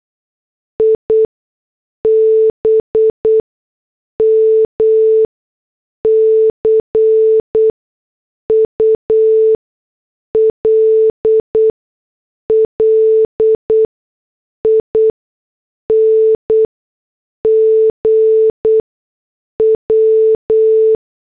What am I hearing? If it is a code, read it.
Morse code, heard as a keyed tone, about 8 wpm, spaced as for 6 wpm, 440 Hz. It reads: IBM CULLING W
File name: morse / headlines